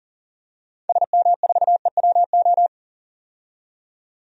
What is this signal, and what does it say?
Morse code, heard as a keyed tone, about 40 wpm, 700 Hz. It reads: SM4EWO
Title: DXCC-100 Call Signs - 6 Characters - 40wpm